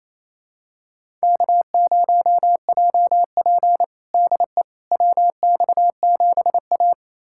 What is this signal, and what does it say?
Morse code, heard as a keyed tone, about 28 wpm, 700 Hz. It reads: K0JP DE WX7A
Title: USA Callsign DE Callsign - 4 Characters - 28wpm